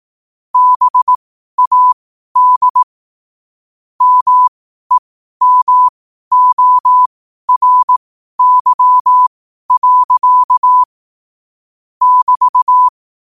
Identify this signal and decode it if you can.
Morse code, heard as a keyed tone, about 18 wpm, spaced as for 12 wpm, 1000 Hz. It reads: BAD MEMORY. <BT>
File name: qotdfiles / 12wpm